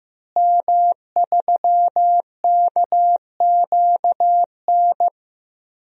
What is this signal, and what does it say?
Morse code, heard as a keyed tone, about 15 wpm, 700 Hz. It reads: M3KQN
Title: DXCC-100 Call Signs - 5 Characters - Repeated 3x - 15wpm